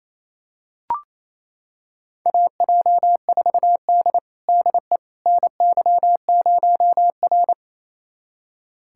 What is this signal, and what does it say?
Morse code, heard as a keyed tone, about 28 wpm, 700 Hz. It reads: AJ4D DE NY0R
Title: USA Callsign DE Callsign - 4 Characters - 28wpm